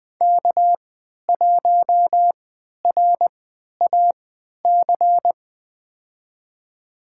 Morse code, heard as a keyed tone, about 20 wpm, spaced as for 11 wpm, 700 Hz. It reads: K1RAC
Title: USA Call Signs - 5 Characters - 3x Character Spacing - 20wpm